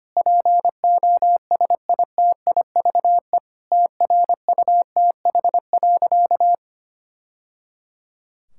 Morse code, heard as a keyed tone, about 25 wpm, 700 Hz. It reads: POSITIVE TRUTH.
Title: Sentences from Top 1000 Words - Review Part 5 - 25wpm